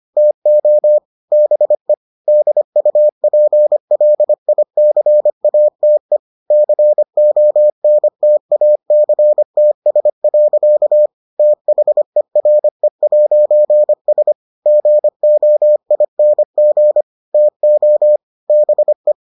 Morse code, heard as a keyed tone, about 25 wpm, 600 Hz. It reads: TO BE DUPLICATE CONTACTS. THERE'S GOING TO BE